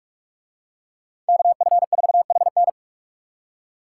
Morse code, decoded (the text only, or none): KF4HN